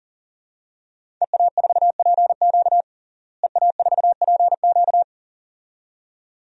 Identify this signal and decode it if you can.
Morse code, heard as a keyed tone, about 40 wpm, 700 Hz. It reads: EA4PQ EA4PQ
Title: DXCC-100 Call Signs - 5 Characters - Repeated 2x - 40wpm